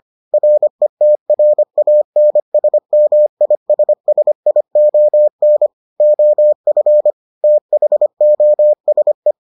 Morse code, heard as a keyed tone, about 25 wpm, 600 Hz. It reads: RETRANSMISSION OF THOSE